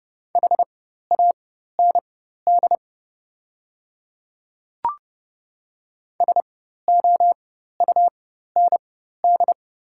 Morse code, heard as a keyed tone, about 30 wpm, spaced as for 14 wpm, 690 Hz. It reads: HAND SOUND
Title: 4x Character Spacing - Top 200 Words - 30wpm